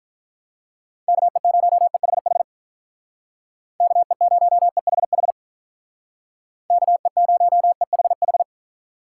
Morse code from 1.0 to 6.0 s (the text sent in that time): KE0EHH KE0EHH